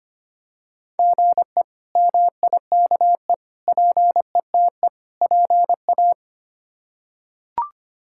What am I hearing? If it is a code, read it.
Morse code, heard as a keyed tone, about 25 wpm, 700 Hz. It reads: GE MIKE PETE PA
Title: K1USN SST QSO Practice - 25wpm